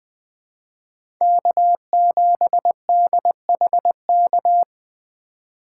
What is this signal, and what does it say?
Morse code, heard as a keyed tone, about 20 wpm, 700 Hz. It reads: K7DHK
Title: USA Call Signs - 5 Characters - 3x Character Spacing - 20wpm